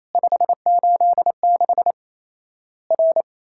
Morse code, heard as a keyed tone, about 27 wpm, 700 Hz. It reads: 586 R
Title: Worked All Europe DX Contest - QTC Exchange - 28wpm